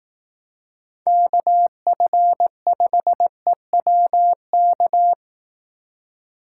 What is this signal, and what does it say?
Morse code, heard as a keyed tone, about 18 wpm, 700 Hz. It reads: KF5EWK
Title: Speed Racing - USA 2x3 Call Signs 15wpm